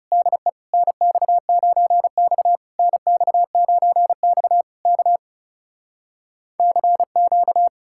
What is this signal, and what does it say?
Morse code, heard as a keyed tone, about 38 wpm, 700 Hz. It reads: DE NX9X NX9X K CQ
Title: Speed-Racing Common QSO Phrases - 30wpm